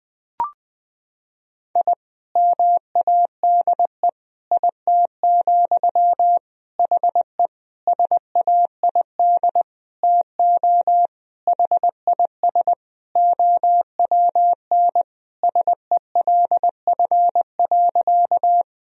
Morse code, heard as a keyed tone, about 20 wpm, 700 Hz. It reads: I MADE IT, HE SAID TO HIS OWN SELF.
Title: Sentences from Top 1000 Words - Review Part 1 - 20wpm